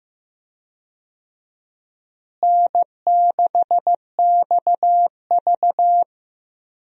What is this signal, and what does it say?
Morse code, heard as a keyed tone, about 15 wpm, 700 Hz. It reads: N6XV
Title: USA Call Signs - 4 Characters - Repeated 3x - 15wpm